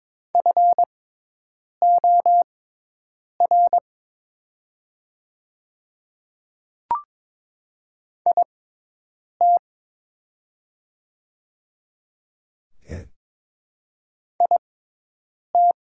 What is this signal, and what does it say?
Morse code, heard as a keyed tone, about 22 wpm, spaced as for 8 wpm, 700 Hz. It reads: FOR IT IT
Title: 6x Character Spacing - Top 100 Words - 22wpm